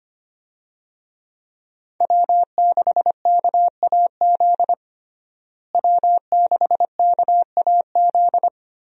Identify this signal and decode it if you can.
Morse code, heard as a keyed tone, about 25 wpm, 700 Hz. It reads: W6KAZ W6KAZ
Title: USA Call Signs - 5 Characters - Repeated 3x - 25wpm